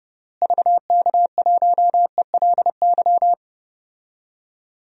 Morse code, heard as a keyed tone, about 30 wpm, 700 Hz. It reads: VK1ELY
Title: DXCC-100 Call Signs - 6 Characters - Repeated 3x - 30wpm